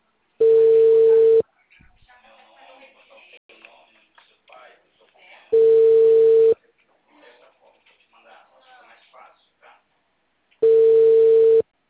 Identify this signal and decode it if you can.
Ringback tone (China, 450 Hz), with speech.
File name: out- -2004